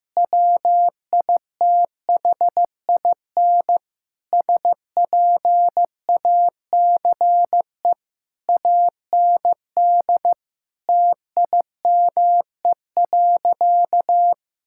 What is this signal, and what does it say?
Morse code, heard as a keyed tone, about 15 wpm, 700 Hz. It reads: WITHIN SPACE AND TIME.
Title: Sentences from Top 1000 Words - Review Part 5 - 15wpm